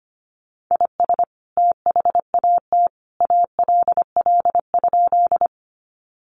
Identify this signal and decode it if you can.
Morse code, heard as a keyed tone, about 25 wpm, 700 Hz. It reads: IS THAT ALL?
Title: Sentences from Top 100 Words 25wpm